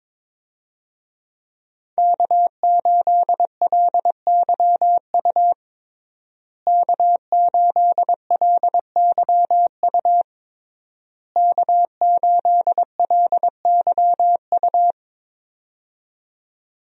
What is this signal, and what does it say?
Morse code, heard as a keyed tone, about 22 wpm, 700 Hz. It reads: K8LYU K8LYU K8LYU
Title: USA Call Signs - 5 Characters - Repeated 3x - 22wpm